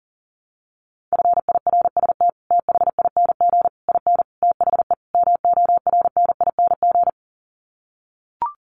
Morse code, heard as a keyed tone, about 40 wpm, 700 Hz. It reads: FIRST THING IN THE MORNING
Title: Sets of 5 Words Encore2 40wpm